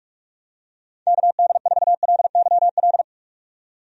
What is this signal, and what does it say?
Morse code, heard as a keyed tone, about 46 wpm, 700 Hz. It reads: KD4LYL
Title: USA Call Signs - 6 Characters - 45wpm